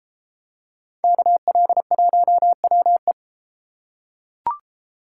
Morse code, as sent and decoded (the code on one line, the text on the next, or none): -.- .-.. .---- .-- .
KL1WE